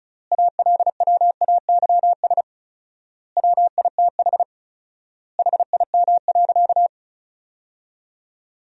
Morse code, heard as a keyed tone, about 36 wpm, 700 Hz. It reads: ALWAYS WITH HIM.
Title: 4x Word-Spacing - Sentences from Top 400 Words - 35wpm